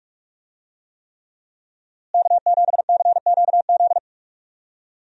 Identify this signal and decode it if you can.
Morse code, heard as a keyed tone, about 46 wpm, 700 Hz. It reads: K7CQZ
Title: USA Call Signs - 5 Characters - Repeated 3x - 45wpm